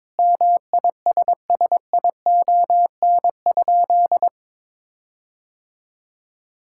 Morse code, heard as a keyed tone, about 22 wpm, 700 Hz. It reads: MISSION?
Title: Sentences from Top 1600 Words 22wpm